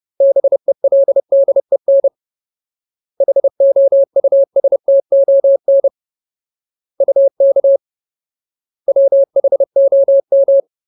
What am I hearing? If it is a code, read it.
Morse code, heard as a keyed tone, about 30 wpm, 550 Hz. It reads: BELDEN HOUSTON UK WHOM
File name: Ultimate Word List / 30WPM Ultimate